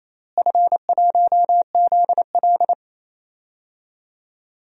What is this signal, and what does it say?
Morse code, heard as a keyed tone, about 28 wpm, 700 Hz. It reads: F1ZL
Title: DXCC-100 Call Signs - 4 Characters - Repeated 2x - 28wpm